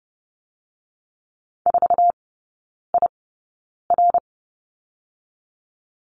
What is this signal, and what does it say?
Morse code, heard as a keyed tone, about 30 wpm, spaced as for 9 wpm, 700 Hz. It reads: 4IR
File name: Lesson 19e - Letter w - Three Characters 30wpm